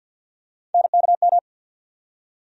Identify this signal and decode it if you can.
Morse code, heard as a keyed tone, about 47 wpm, 700 Hz. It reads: NKM
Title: USA Call Sign Postfixes - 3 Letters 50wpm